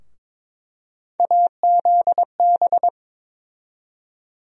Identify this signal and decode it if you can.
Morse code, heard as a keyed tone, about 22 wpm, 700 Hz. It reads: AZB
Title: USA Call Sign Postfixes - 3 Letters 22wpm